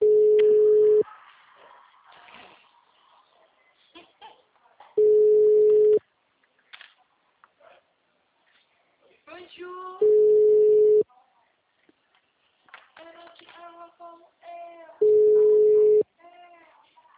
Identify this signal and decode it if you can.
Ringback tone (Europe, 425 Hz), with speech.